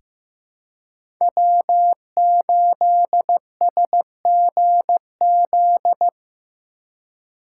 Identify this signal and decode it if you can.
Morse code, heard as a keyed tone, about 15 wpm, 700 Hz. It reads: W8SGZ
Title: USA Call Signs - 5 Characters - 3x Character Spacing - 15wpm